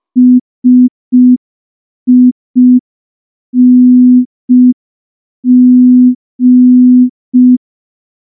Morse code, heard as a keyed tone, about 5 wpm, 250 Hz. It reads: SING
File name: morse / morse-training-05WPM